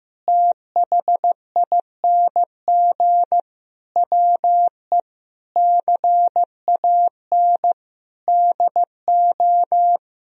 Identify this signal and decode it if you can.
Morse code, heard as a keyed tone, about 15 wpm, 700 Hz. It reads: THING WE CAN DO
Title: Sentences from Top 200 Words and Abbreviations 15wpm